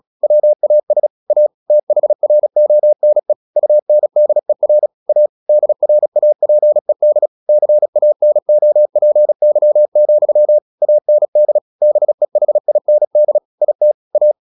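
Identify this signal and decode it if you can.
Morse code, heard as a keyed tone, about 36 wpm, 600 Hz. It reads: WAS A THRONE UNDER A DRAPED CANOPY, AND BEHIND IT A
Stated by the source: Morse practice at 36 wpm, text of Ken Follett's novel Never, from CW med Gnister.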